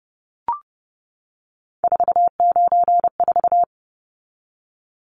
Morse code, heard as a keyed tone, about 30 wpm, 700 Hz. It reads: 494
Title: Numbers - 3-Digit 30wpm